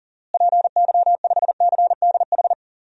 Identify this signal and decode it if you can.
Morse code, heard as a keyed tone, about 39 wpm, 700 Hz. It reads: PY5CDH
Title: DXCC-100 Call Signs - 6 Characters - Repeated 3x - 40wpm